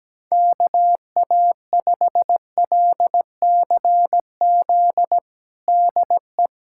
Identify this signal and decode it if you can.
Morse code, heard as a keyed tone, about 17 wpm, 700 Hz. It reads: KA5LCZ DE
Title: USA Callsign DE Callsign - 6 Characters - 17wpm